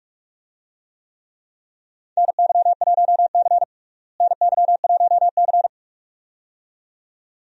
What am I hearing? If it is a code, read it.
Morse code, heard as a keyed tone, about 46 wpm, 700 Hz. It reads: NY1C NY1C
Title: USA Call Signs - 4 Characters - Repeated 2x - 45wpm